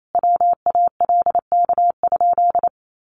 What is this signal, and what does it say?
Morse code, heard as a keyed tone, about 28 wpm, 700 Hz. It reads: WALK?
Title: Sentences from Top 1100 Words 28wpm